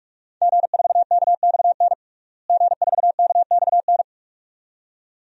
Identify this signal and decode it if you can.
Morse code, heard as a keyed tone, about 46 wpm, 700 Hz. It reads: G4KXN G4KXN
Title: DXCC-100 Call Signs - 5 Characters - Repeated 2x - 45wpm